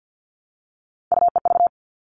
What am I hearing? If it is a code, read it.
Morse code, heard as a keyed tone, about 50 wpm, 700 Hz. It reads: UEV
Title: All Cut Numbers - 3-Digit 50wpm